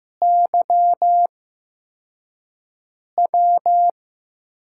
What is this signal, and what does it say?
Morse code, heard as a keyed tone, about 15 wpm, spaced as for 4 wpm, 700 Hz. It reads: YW
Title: USA Call Signs - 8x Character Spacing - 15wpm